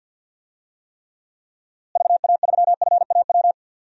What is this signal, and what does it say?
Morse code, heard as a keyed tone, about 48 wpm, 700 Hz. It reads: VA3FAW
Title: DXCC-100 Call Signs - 6 Characters - Repeated 3x - 50wpm